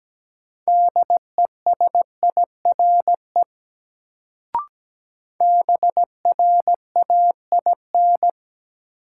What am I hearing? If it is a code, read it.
Morse code, heard as a keyed tone, about 17 wpm, 700 Hz. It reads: DESIRE BRAIN
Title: Sentences from Top 1100 Words - New Word Drill Alt - 17wpm